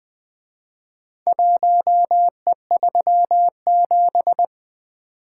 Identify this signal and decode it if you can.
Morse code, heard as a keyed tone, about 24 wpm, 700 Hz. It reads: 1E37
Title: Speed-Racing - Four Letter-Number Combinations - 20wpm